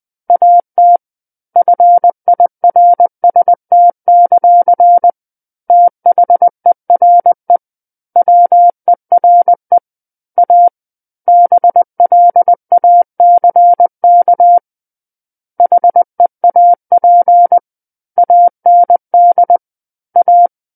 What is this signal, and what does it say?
Morse code, heard as a keyed tone, about 20 wpm, 700 Hz. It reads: AT FIRST; THERE WERE A BLACK HEAP AND A
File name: 20 WPM